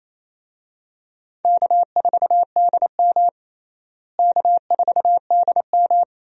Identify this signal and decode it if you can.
Morse code, heard as a keyed tone, about 28 wpm, 700 Hz. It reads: K4DM K4DM
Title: USA Call Signs - 4 Characters - Repeated 2x - 28wpm